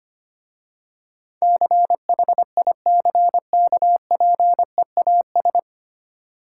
Morse code, heard as a keyed tone, about 25 wpm, 700 Hz. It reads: CHICKPEAS
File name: practice / 25wpm / Difficult Words with Hints - 25wpm